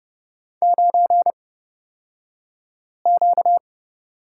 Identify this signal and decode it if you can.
Morse code, heard as a keyed tone, about 30 wpm, spaced as for 5 wpm, 700 Hz. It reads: 9Q
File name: course / 30:10wpm / 235 Lesson 38d - Letter q - Two Characters 30wpm